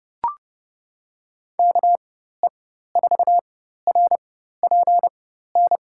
Morse code, heard as a keyed tone, about 30 wpm, spaced as for 14 wpm, 700 Hz. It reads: KE4RPN
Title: USA Call Signs - 6 Characters - 4x Character Spacing - 30wpm